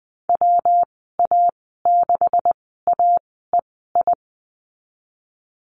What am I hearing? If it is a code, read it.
Morse code, heard as a keyed tone, about 20 wpm, spaced as for 14 wpm, 700 Hz. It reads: WA6AEI
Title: 140 Lesson 23g - Letter f - Call Signs 20wpm